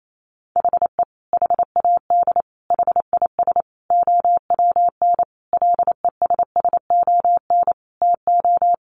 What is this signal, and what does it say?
Morse code, heard as a keyed tone, about 28 wpm, 700 Hz. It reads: HE HAD HIS OWN LESSON TO
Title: Sentences from Top 1200 Words 28wpm